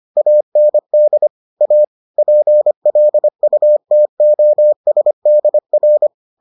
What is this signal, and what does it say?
Morse code, heard as a keyed tone, about 25 wpm, 600 Hz. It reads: AND A PLUTOSDR